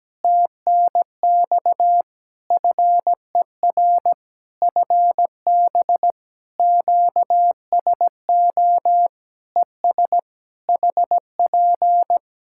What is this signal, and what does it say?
Morse code, heard as a keyed tone, about 17 wpm, 700 Hz. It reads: TNX FER FB QSO ES HP
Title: Practice QSO - Exchange 4 - Ending - 17wpm